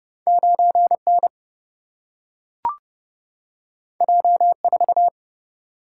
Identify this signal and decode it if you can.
Morse code, heard as a keyed tone, about 30 wpm, 700 Hz. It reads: 9 N J 4
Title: Single-Letter Single-Number 30wpm